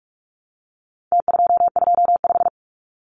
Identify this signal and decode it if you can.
Morse code, heard as a keyed tone, about 44 wpm, 700 Hz. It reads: T225
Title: Numbers - Contest Style - 4-Digit 45wpm